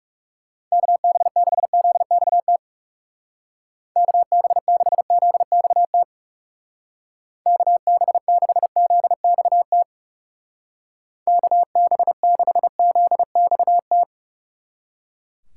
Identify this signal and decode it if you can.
Morse code, heard as a keyed tone, about 37 wpm, 700 Hz. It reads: KB6ZXT KB6ZXT KB6ZXT KB6ZXT
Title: Speed Racing - USA 2x3 Call Signs 30wpm